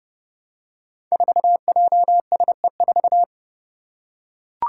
Morse code, heard as a keyed tone, about 30 wpm, 700 Hz. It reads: 4JSE4
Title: Five Challenging Letter-Number Combinations 30wpm